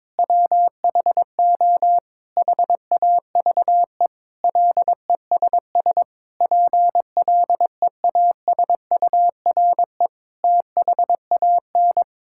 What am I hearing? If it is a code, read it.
Morse code, heard as a keyed tone, about 22 wpm, 700 Hz. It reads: WHO HAVE LESS PLEASURE THAN
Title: Sentences from Top 1600 Words 22wpm